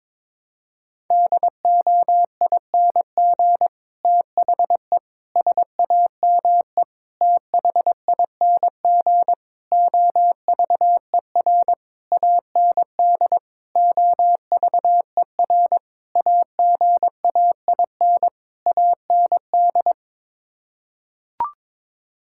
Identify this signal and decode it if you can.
Morse code, heard as a keyed tone, about 22 wpm, 700 Hz. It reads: DOING THE SAME THING OVER AND OVER AGAIN AND
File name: Sets of 9 Words - 22wpm